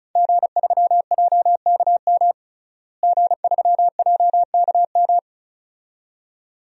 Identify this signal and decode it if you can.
Morse code, heard as a keyed tone, about 36 wpm, 700 Hz. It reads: G3JKM G3JKM
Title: DXCC-100 Call Signs - 5 Characters - Repeated 2x - 35wpm